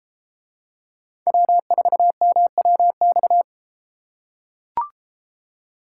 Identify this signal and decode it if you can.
Morse code, heard as a keyed tone, about 32 wpm, 700 Hz. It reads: W4MWX